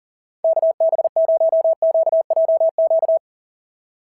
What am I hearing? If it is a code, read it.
Morse code, heard as a keyed tone, about 39 wpm, 690 Hz. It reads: KB0QJQ
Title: SOTA QSO Practice 40wpm